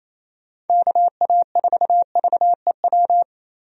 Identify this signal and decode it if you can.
Morse code, heard as a keyed tone, about 28 wpm, 700 Hz. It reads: KA4VEW
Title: USA Call Signs - 6 Characters - 28wpm